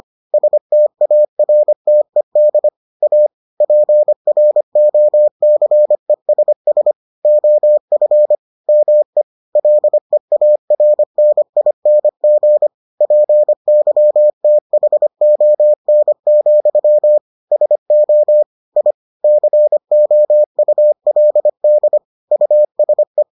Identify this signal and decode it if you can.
Morse code, heard as a keyed tone, about 25 wpm, 600 Hz. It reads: STARTED A PROCESS OF ME LEARNING PYTHON, SO I COULD USE